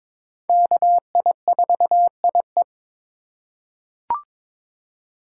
Morse code, heard as a keyed tone, about 22 wpm, 700 Hz. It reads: KI4IE
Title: USA Call Signs - 5 Characters - 4x Character Spacing - 22wpm